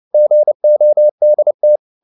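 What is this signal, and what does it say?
Morse code, heard as a keyed tone, about 29 wpm, 600 Hz. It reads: GODT
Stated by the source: Fast (QRQ) Morse at 29 wpm, from CW med Gnister.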